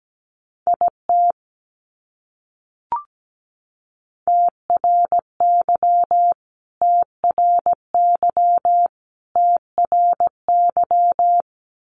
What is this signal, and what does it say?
Morse code, heard as a keyed tone, about 17 wpm, 700 Hz. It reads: IT TRY TRY TRY
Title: Top 200 Words Repeated 17wpm